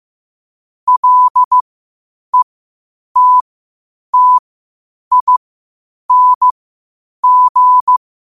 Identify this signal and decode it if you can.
Morse code, heard as a keyed tone, about 15 wpm, spaced as for 8 wpm, 1000 Hz. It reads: LETTING
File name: qotdfiles / 8wpm